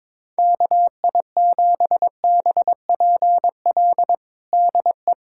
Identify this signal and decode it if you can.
Morse code, heard as a keyed tone, about 22 wpm, 700 Hz. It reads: KI7BPL DE
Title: USA Callsign DE Callsign - 6 Characters - 22wpm